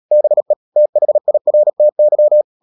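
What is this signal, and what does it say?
Morse code, heard as a keyed tone, about 37 wpm, 600 Hz. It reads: BE THIRTY
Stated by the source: Morse code practice at 35–39 wpm sending English text of Ken Follett's novel Never, from CW med Gnister.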